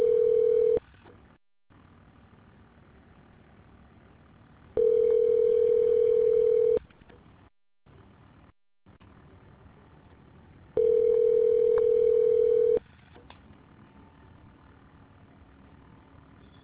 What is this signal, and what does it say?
Ringback tone (North America), 440 + 480 Hz.